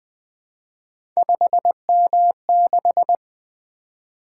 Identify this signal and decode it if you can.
Morse code, heard as a keyed tone, about 20 wpm, 700 Hz. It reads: 5M6
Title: Three Letter-Number Combinations - 20wpm